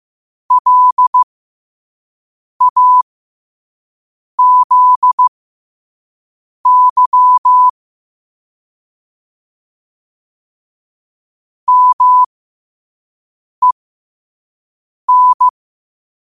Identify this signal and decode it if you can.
Morse code, heard as a keyed tone, about 15 wpm, spaced as for 5 wpm, 1000 Hz. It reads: LAZY MEN
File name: qotdfiles / 5wpm